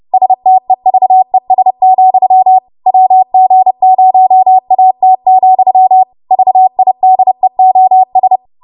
Morse code, heard as a keyed tone, about 30 wpm, 750 Hz. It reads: STEVES, WG0AT, VIDEOS
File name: Morse / 30WPM